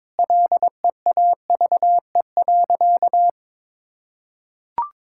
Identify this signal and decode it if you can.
Morse code, heard as a keyed tone, about 22 wpm, 700 Hz. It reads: LEAVE.
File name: practice / 22wpm / Sentences with 4-Word Phrases - 22wpm